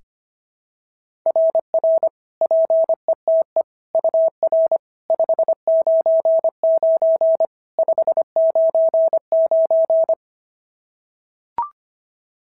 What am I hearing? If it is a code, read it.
Morse code, heard as a keyed tone, about 22 wpm, 650 Hz. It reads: RR PETE UR 599 599
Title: SOTA QSO Practice 25wpm